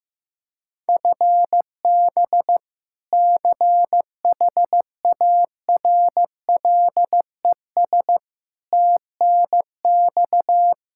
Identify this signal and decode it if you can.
Morse code, heard as a keyed tone, about 16 wpm, 700 Hz. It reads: FB CHARLES TNX
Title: Practice QSO - Exchange 3 - More Info - 15wpm